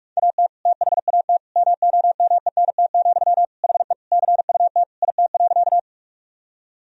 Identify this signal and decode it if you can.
Morse code, heard as a keyed tone, about 46 wpm, 700 Hz. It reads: AT THAT MOMENT, HE CUT IT.
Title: Sentences from Top 1000 Words - Review Part 1 - 45wpm